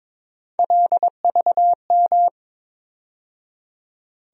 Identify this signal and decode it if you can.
Morse code, heard as a keyed tone, about 22 wpm, 700 Hz. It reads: LVM
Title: USA Call Sign Postfixes - 3 Letters 22wpm